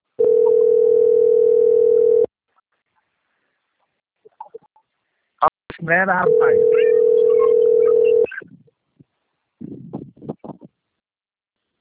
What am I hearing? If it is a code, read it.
Ringback tone (North America), 440 + 480 Hz, with speech.
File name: out- - - - - .811